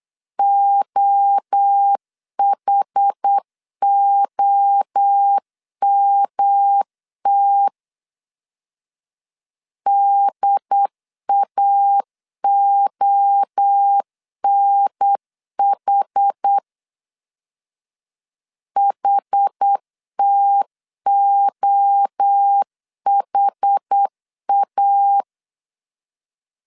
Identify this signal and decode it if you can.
Morse code, heard as a keyed tone, about 8 wpm, 800 Hz. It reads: OHOMT DAONH HTOHA
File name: CW / DET / 1 3 1500C S A 8wpm 4229